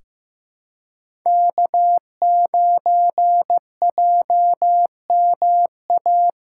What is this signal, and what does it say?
Morse code, heard as a keyed tone, about 15 wpm, 700 Hz. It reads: K9JMA
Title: USA Callsign DE Callsign - 5 Characters - 15wpm